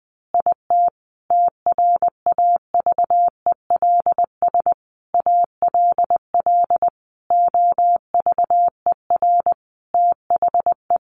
Morse code, heard as a keyed tone, about 20 wpm, 700 Hz. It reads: IT TRAVELS ALL OVER THE
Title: Additional Easy Phrases 20wpm